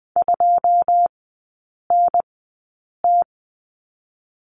Morse code, heard as a keyed tone, about 20 wpm, spaced as for 8 wpm, 700 Hz. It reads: 2NT
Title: 081 Lesson 14e - Number 2 - Three Characters 20wpm